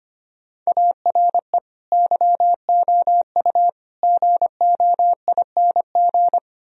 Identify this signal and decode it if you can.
Morse code, heard as a keyed tone, about 31 wpm, 700 Hz. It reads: ARE YOU GOING
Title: Speed-Racing - Sets of 4 Words 25wpm